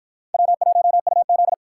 Morse code, heard as a keyed tone, about 61 wpm, 700 Hz. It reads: W1UZ